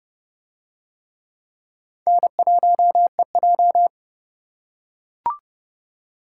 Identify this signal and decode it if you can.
Morse code, heard as a keyed tone, about 29 wpm, 700 Hz. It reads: N1EJ